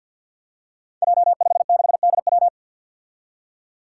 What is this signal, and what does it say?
Morse code, heard as a keyed tone, about 53 wpm, 700 Hz. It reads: J56DW
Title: Five Challenging Letter-Number Combinations 50wpm